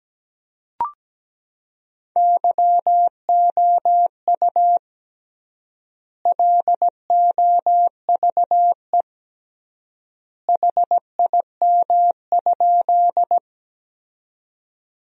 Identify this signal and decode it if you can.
Morse code, heard as a keyed tone, about 17 wpm, 700 Hz. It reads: YOU LOVE HIM?
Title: 3x Word-Spacing - Sentences from Top 400 Words - 17wpm